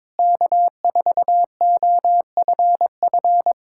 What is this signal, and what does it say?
Morse code, heard as a keyed tone, about 22 wpm, 700 Hz. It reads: K4OFF
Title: Speed-Racing Common QSO Phrases - 22wpm